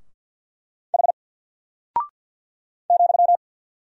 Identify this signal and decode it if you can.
Morse code, heard as a keyed tone, about 48 wpm, 700 Hz. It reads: H ,
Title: Single Character (Letter:Number:Punctuation:Prosign) 50wpm